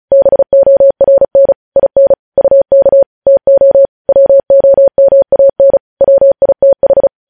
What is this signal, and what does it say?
Morse code, heard as a keyed tone, about 36 wpm, 550 Hz. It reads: BORN IN UK TO WOMAN WITH
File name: podcast / News headlines-1x-35x35